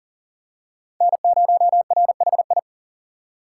Morse code, heard as a keyed tone, about 40 wpm, 700 Hz. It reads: N0RHI
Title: USA Call Signs - 5 Characters - 6x Character Spacing - 40wpm